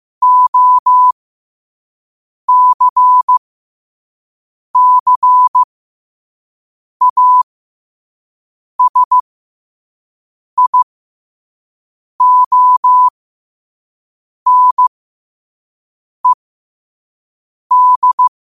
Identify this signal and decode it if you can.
Morse code, heard as a keyed tone, about 15 wpm, spaced as for 5 wpm, 1000 Hz. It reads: OCCASIONED
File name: qotdfiles / 5wpm